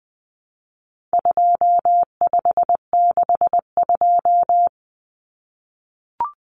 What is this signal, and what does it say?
Morse code, heard as a keyed tone, about 20 wpm, 700 Hz. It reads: 2562
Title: Common Cut Numbers - 4-Digit 20wpm